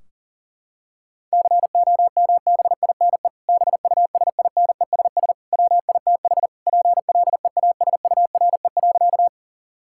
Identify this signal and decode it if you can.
Morse code, heard as a keyed tone, about 40 wpm, 700 Hz. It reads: COMBINE BUSINESS WITH PLEASURE.
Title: Sentences from Top 1600 Words 40wpm